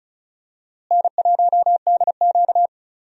Morse code, heard as a keyed tone, about 36 wpm, 700 Hz. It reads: N1DQ
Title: USA Call Signs - 4 Characters - Repeated 2x - 35wpm